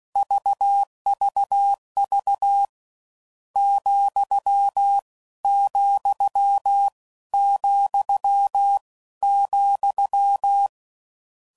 Morse code, heard as a keyed tone, about 16 wpm, spaced as for 13 wpm, 800 Hz. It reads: VVV ,,,,